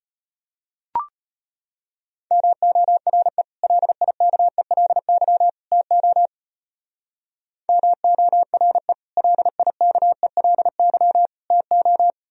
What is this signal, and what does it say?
Morse code, heard as a keyed tone, about 36 wpm, 700 Hz. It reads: MORE LIKELY TO MORE LIKELY TO